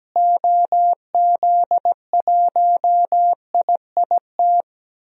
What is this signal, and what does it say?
Morse code, heard as a keyed tone, about 17 wpm, 700 Hz. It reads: OZ1IIT
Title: DXCC-100 Call Signs - 6 Characters - Repeated 3x - 17wpm